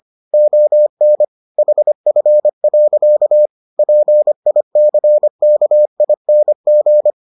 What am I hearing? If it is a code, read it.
Morse code, heard as a keyed tone, about 25 wpm, 600 Hz. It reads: ON HF. PICKING